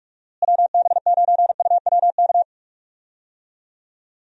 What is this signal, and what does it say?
Morse code, heard as a keyed tone, about 45 wpm, 700 Hz. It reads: WB9UWK
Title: USA Call Signs - 6 Characters - Repeated 2x - 45wpm